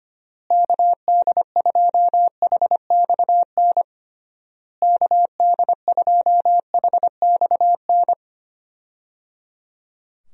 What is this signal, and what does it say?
Morse code, heard as a keyed tone, about 25 wpm, 700 Hz. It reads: KD2HXN KD2HXN
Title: USA Call Signs - 6 Characters - Repeated 3x - 25wpm